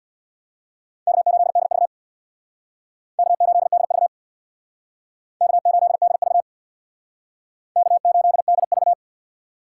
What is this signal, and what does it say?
Morse code, heard as a keyed tone, about 64 wpm, 700 Hz. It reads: K8DV K8DV K8DV K8DV